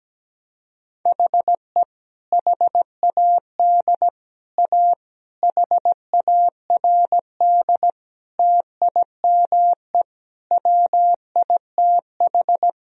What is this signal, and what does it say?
Morse code, heard as a keyed tone, about 17 wpm, 700 Hz. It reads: HE HAD A HARD TIME WITH
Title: Sentences from Top 1000 Words - Review Part 1 - 17wpm